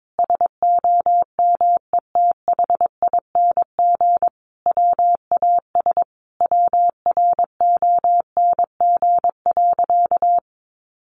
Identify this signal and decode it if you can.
Morse code, heard as a keyed tone, about 22 wpm, 700 Hz. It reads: SOMETHING WAS WRONG.
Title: Sentences from Top 1300 Words 22wpm